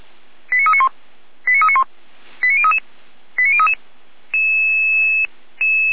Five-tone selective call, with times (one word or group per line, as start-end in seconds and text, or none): 0.5-0.9 89391
1.5-1.8 89391
2.4-2.8 89033
3.4-3.7 89033